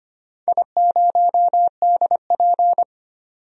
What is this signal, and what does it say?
Morse code, heard as a keyed tone, about 25 wpm, 700 Hz. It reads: I0DP
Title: DXCC-100 Call Signs - 4 Characters - 25wpm